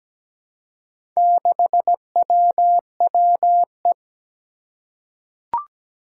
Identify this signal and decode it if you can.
Morse code, heard as a keyed tone, about 17 wpm, 700 Hz. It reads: 6WWE
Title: Four Challenging Letter-Number Combinations 17wpm